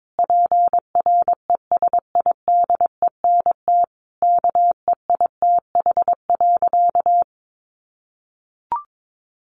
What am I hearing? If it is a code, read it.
Morse code, heard as a keyed tone, about 22 wpm, 700 Hz. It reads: PRESIDENT KEITH.
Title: A Sound Of Thunder 22wpm